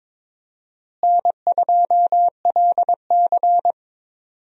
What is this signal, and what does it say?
Morse code, heard as a keyed tone, about 21 wpm, 700 Hz. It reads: N2LC
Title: USA Call Signs - 4 Characters - Repeated 3x - 22wpm